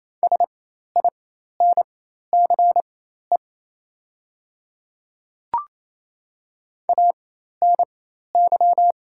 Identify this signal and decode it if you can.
Morse code, heard as a keyed tone, about 28 wpm, spaced as for 13 wpm, 690 Hz. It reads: SINCE ANY
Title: 4x Character Spacing - Top 300 Words - 28wpm